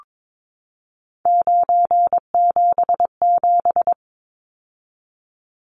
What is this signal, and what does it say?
Morse code, heard as a keyed tone, about 22 wpm, 700 Hz. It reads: 977
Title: Numbers - 3-Digit 22wpm